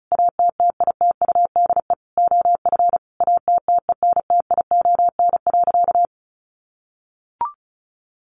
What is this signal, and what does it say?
Morse code, heard as a keyed tone, about 34 wpm, 700 Hz. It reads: ATTITUDE OF ATTENTION.
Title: Sentences from Top 1300 Words 35wpm